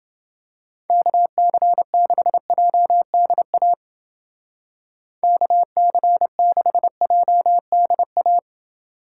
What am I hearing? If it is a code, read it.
Morse code, heard as a keyed tone, about 29 wpm, 700 Hz. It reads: KC6JDA KC6JDA